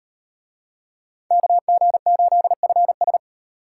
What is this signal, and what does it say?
Morse code, heard as a keyed tone, about 38 wpm, 700 Hz. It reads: KG8FS